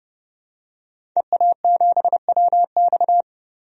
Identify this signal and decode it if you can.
Morse code, heard as a keyed tone, about 32 wpm, 700 Hz. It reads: EA7WX
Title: DXCC-100 Call Signs - 5 Characters - 30wpm